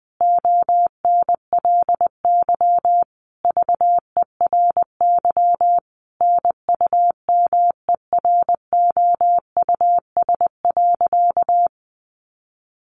Morse code, heard as a keyed tone, about 20 wpm, 700 Hz. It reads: ONLY VERY NUMEROUS.
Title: Sentences from Top 1400 Words 20wpm